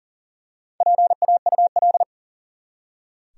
Morse code, heard as a keyed tone, about 40 wpm, 700 Hz. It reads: PAUL
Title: USA Common Names - Max 4 Letters - 40wpm